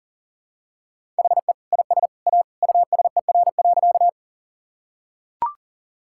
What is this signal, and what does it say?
Morse code, heard as a keyed tone, about 40 wpm, 700 Hz. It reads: HE IS A USER.
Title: Sentences from Top 1000 Words - Review Part 5 - 40wpm